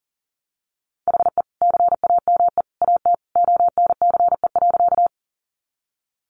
Morse code, heard as a keyed tone, about 40 wpm, 700 Hz. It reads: HE CAME AT ONCE.
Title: Sentences from Top 300 Words 40wpm